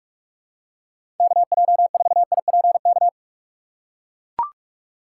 Morse code, heard as a keyed tone, about 41 wpm, 700 Hz. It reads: KJ4IPK
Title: USA Call Signs - 6 Characters - 2x Character Spacing - 45wpm